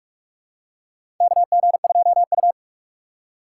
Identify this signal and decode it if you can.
Morse code, heard as a keyed tone, about 46 wpm, 700 Hz. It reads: KG2U
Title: USA Call Signs - 4 Characters - 2x Character Spacing - 45wpm